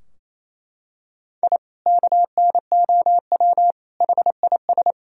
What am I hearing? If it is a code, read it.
Morse code, heard as a keyed tone, about 28 wpm, 700 Hz. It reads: I KNOW HIS
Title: Sentences from Top 1000 Words - Review Part 3 - 28wpm